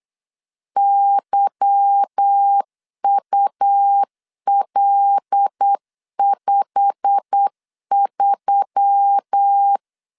Morse code, heard as a keyed tone, about 8 wpm, 800 Hz. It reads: YUL53
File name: LEZIONI / 9 2 8wpm 3437